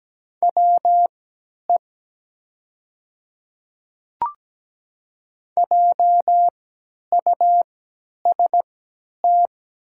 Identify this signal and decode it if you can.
Morse code, heard as a keyed tone, about 17 wpm, spaced as for 10 wpm, 700 Hz. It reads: WE JUST
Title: 3x Character Spacing - Top 500 Words - 17wpm